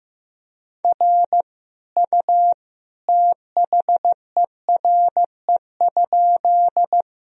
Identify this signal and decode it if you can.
Morse code, heard as a keyed tone, about 15 wpm, 700 Hz. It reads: R U THERE?
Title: Sentences from Top 100 Words and Abbreviations 15wpm